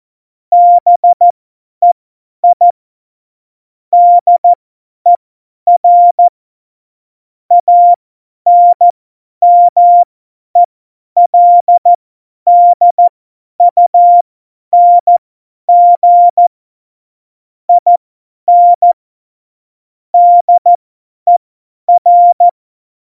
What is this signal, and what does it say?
Morse code, heard as a keyed tone, about 14 wpm, spaced as for 10 wpm, 700 Hz. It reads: BEI DER ANMELDUNG IN DER
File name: Morsekurs / Witze / Witz 12 Mai 2020, 50 F70